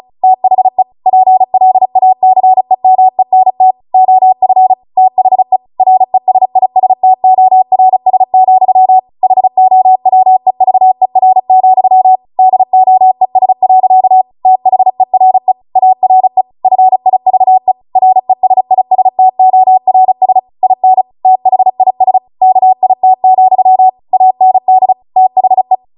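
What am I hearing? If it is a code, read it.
Morse code, heard as a keyed tone, about 36 wpm, 750 Hz. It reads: THE PLACEMENT OF THE RESISTORS, HOWEVER, DOES. THERE ARE FIVE RESISTORS IN THIS KIT, AND THE
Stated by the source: Amateur radio Morse code practice archive transmission at 35 wpm.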